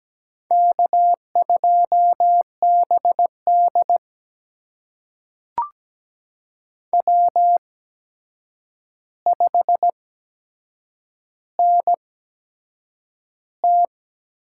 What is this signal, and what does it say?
Morse code, heard as a keyed tone, about 17 wpm, spaced as for 7 wpm, 700 Hz. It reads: K2BD W5NT
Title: USA Call Signs - 4 Characters - 8x Character Spacing - 17wpm